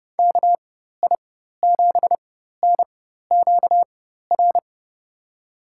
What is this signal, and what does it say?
Morse code, heard as a keyed tone, about 32 wpm, spaced as for 14 wpm, 700 Hz. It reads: KI7NQR
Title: USA Call Signs - 6 Characters - 4x Character Spacing - 30wpm